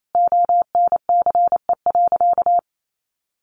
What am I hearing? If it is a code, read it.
Morse code, heard as a keyed tone, about 28 wpm, 700 Hz. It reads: ONCE.
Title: Sentences from Top 300 Words 28wpm